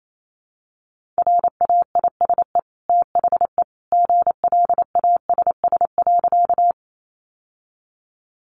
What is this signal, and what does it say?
Morse code, heard as a keyed tone, about 28 wpm, 700 Hz. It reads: RAISE THE GLASS.
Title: Sentences from Top 1400 Words 28wpm